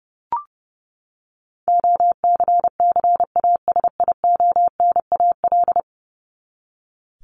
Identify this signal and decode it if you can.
Morse code, heard as a keyed tone, about 30 wpm, 700 Hz. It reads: OCCASIONAL
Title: LY Suffix - Top 100 Words with Base Words 30wpm